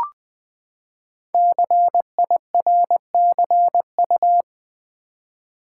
Morse code, heard as a keyed tone, about 20 wpm, 700 Hz. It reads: CIRCU
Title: Build A Word - 5 Letters At A Time - 12-Letter Words - 20wpm